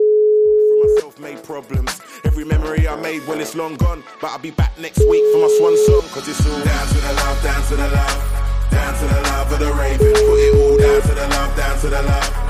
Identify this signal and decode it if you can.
Ringback tone (Europe, 425 Hz).